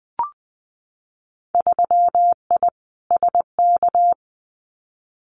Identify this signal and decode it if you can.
Morse code, heard as a keyed tone, about 20 wpm, 700 Hz. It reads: 3I SK
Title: Winter Field Day Exchanges Encore 20wpm